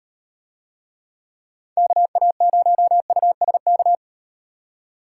Morse code, heard as a keyed tone, about 38 wpm, 700 Hz. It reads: KA0USK